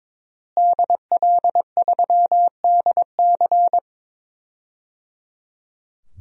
Morse code, heard as a keyed tone, about 23 wpm, 700 Hz. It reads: DL3DC
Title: DXCC-100 Call Signs - 5 Characters - Repeated 3x - 22wpm